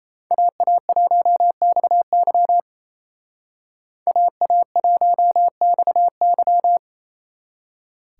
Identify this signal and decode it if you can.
Morse code, heard as a keyed tone, about 34 wpm, 700 Hz. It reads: AA1XY AA1XY
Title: Speed Racing - USA 2x2 Call Signs 28wpm